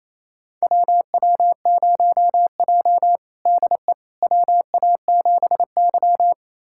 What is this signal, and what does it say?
Morse code, heard as a keyed tone, about 28 wpm, 700 Hz. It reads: WW0J DE WA7Y
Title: USA Callsign DE Callsign - 4 Characters - 28wpm